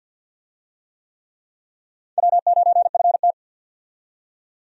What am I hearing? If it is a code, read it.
Morse code, heard as a keyed tone, about 49 wpm, 700 Hz. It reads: W9FT
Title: USA Call Signs - 4 Characters - 50wpm